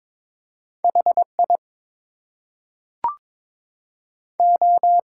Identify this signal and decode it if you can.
Morse code, heard as a keyed tone, about 22 wpm, 700 Hz. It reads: HI O
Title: Two Letter Words - 22wpm